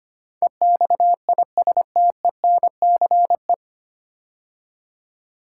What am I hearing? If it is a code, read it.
Morse code, heard as a keyed tone, about 25 wpm, 700 Hz. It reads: EXISTENCE
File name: practice / 25wpm / 9 Letter Words - First 3 Letters Given - 25wpm